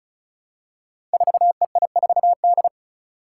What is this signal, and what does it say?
Morse code, heard as a keyed tone, about 36 wpm, 700 Hz. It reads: 4EI4D
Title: Five Challenging Letter-Number Combinations 35wpm